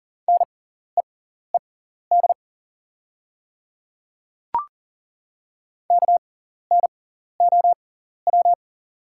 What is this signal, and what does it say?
Morse code, heard as a keyed tone, about 38 wpm, spaced as for 14 wpm, 690 Hz. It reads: NEED KNOW
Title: 6x Character Spacing - Top 300 Words - 40wpm